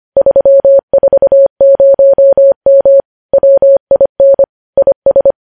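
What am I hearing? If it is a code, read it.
Morse code, heard as a keyed tone, about 25 wpm, 550 Hz. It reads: 340M WIN IS
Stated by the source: Morse code news headlines at 25 words per minute.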